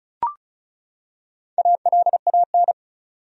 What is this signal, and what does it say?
Morse code, heard as a keyed tone, about 34 wpm, 700 Hz. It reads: ALAN
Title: USA Common Names - 35wpm